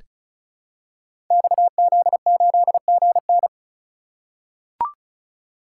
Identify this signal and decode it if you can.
Morse code, heard as a keyed tone, about 36 wpm, 700 Hz. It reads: XZ8GN
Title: Code Groups - 5 Letter:Numbers - 35wpm